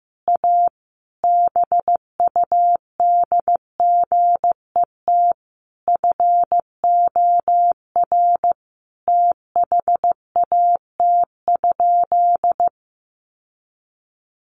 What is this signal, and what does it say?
Morse code, heard as a keyed tone, about 15 wpm, 700 Hz. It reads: A BUDGET FOR THAT?
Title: Sentences from Top 1200 Words 15wpm